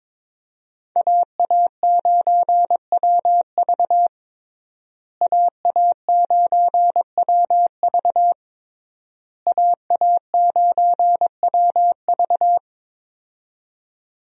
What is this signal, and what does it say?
Morse code, heard as a keyed tone, about 22 wpm, 700 Hz. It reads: AA9WV AA9WV AA9WV
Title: USA Call Signs - 5 Characters - Repeated 3x - 22wpm